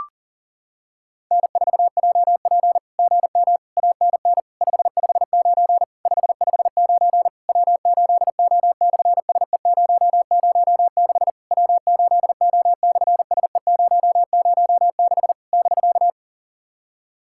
Morse code, heard as a keyed tone, about 40 wpm, 690 Hz. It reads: N4JP GM ANN 559 559 W8O/SE006 W8O/SE006 <BK>
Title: SOTA QSO Practice 40wpm